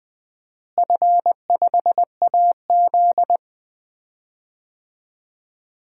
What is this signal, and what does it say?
Morse code, heard as a keyed tone, about 21 wpm, 700 Hz. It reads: F5AZ
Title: DXCC-100 Call Signs - 4 Characters - Repeated 2x - 20wpm